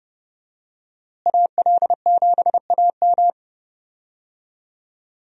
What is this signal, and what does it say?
Morse code, heard as a keyed tone, about 32 wpm, 700 Hz. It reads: AL7AM
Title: USA Call Signs - 5 Characters - Repeated 2x - 30wpm